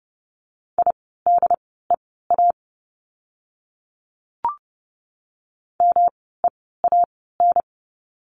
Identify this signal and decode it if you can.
Morse code, heard as a keyed tone, about 30 wpm, spaced as for 17 wpm, 700 Hz. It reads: IDEA MEAN
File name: course / 30:10wpm / 108 Lesson 18f - Letter m - Words 30wpm